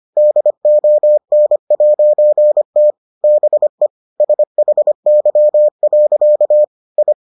Morse code, heard as a keyed tone, about 25 wpm, 600 Hz. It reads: DON'T BE SHY. I